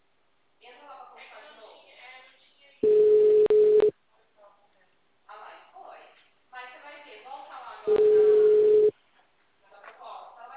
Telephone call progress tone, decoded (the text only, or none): ringback tone